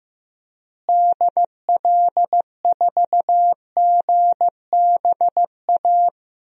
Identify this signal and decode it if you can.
Morse code, heard as a keyed tone, about 15 wpm, 700 Hz. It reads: DL4GBA
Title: DXCC-100 Call Signs - 6 Characters - Repeated 3x - 15wpm